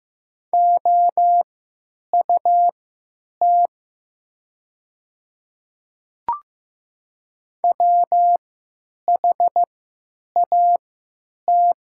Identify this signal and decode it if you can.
Morse code, heard as a keyed tone, about 16 wpm, spaced as for 9 wpm, 700 Hz. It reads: OUT WHAT
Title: 3x Character Spacing - Top 500 Words - 15wpm